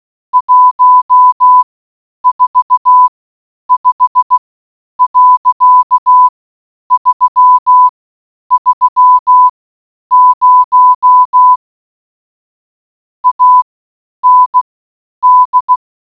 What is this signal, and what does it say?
Morse code, heard as a keyed tone, about 16 wpm, spaced as for 9 wpm, 1000 Hz. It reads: 145.330 AND